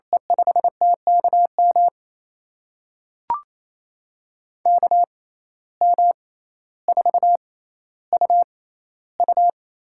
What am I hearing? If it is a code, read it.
Morse code, heard as a keyed tone, about 28 wpm, spaced as for 14 wpm, 700 Hz. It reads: E5TKM KM4UU
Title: USA Call Signs - 6 Characters - 6x Character Spacing - 28wpm